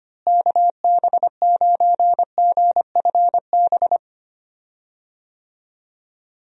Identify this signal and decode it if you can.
Morse code, heard as a keyed tone, about 25 wpm, 700 Hz. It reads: KB9GFB
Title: USA Call Signs - 6 Characters - Repeated 2x - 25wpm